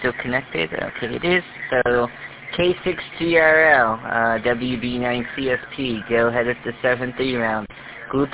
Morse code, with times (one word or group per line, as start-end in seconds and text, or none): none